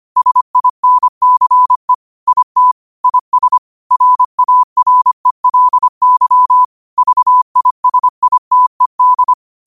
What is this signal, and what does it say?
Morse code, heard as a keyed tone, about 25 wpm, 1000 Hz. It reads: SINCE IT IS RARELY VISITED
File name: qotdfiles / 25wpm